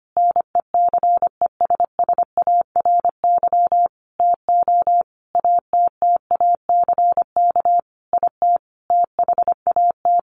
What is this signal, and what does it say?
Morse code, heard as a keyed tone, about 25 wpm, 700 Hz. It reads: NECESSARY TO ATTACK IT THAT
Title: Sentences from Top 1500 Words 25wpm